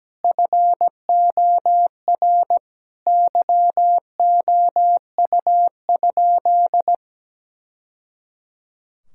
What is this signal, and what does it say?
Morse code, heard as a keyed tone, about 17 wpm, 700 Hz. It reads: FOR YOU?
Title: Sentences from Top 1000 Words - Review Part 5 - 17wpm